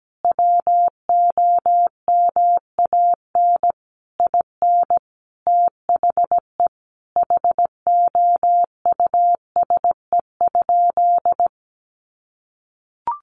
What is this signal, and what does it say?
Morse code, heard as a keyed tone, about 17 wpm, 700 Hz. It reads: WOMAN IN THE HOUSE?
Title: Sentences from Top 1100 Words 17wpm